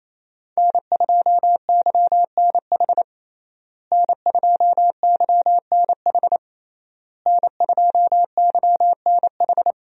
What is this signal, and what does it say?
Morse code, heard as a keyed tone, about 28 wpm, 700 Hz. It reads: N2YNH N2YNH N2YNH
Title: USA Call Signs - 5 Characters - Repeated 3x - 28wpm